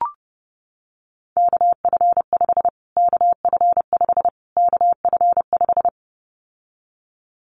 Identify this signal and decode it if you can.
Morse code, heard as a keyed tone, about 30 wpm, 700 Hz. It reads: KF5 KF5 KF5
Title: Callsign Prefixes for Top 25 DXCC - Repeated 3x - 30wpm